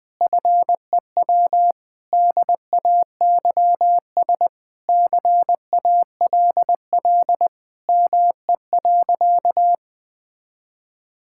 Morse code, heard as a keyed tone, about 20 wpm, 700 Hz. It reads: FEW DAYS CALL ME.
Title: Sentences with 3-Word Phrases - Spoken Beginning - 20wpm